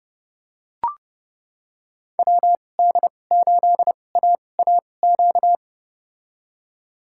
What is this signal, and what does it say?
Morse code, heard as a keyed tone, about 32 wpm, spaced as for 22 wpm, 700 Hz. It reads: WD8AAQ
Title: USA Call Signs - 2x Character Spacing - 30wpm